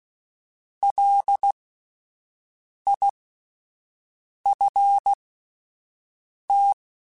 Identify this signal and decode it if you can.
Morse code, heard as a keyed tone, about 16 wpm, spaced as for 5 wpm, 800 Hz. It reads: LIFT
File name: Alphabetical / Alphabetical mp3s / Ls @5